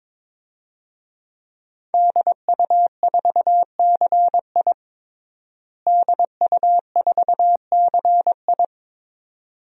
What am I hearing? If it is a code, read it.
Morse code, heard as a keyed tone, about 22 wpm, 700 Hz. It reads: DU4CI DU4CI
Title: DXCC-100 Call Signs - 5 Characters - Repeated 2x - 22wpm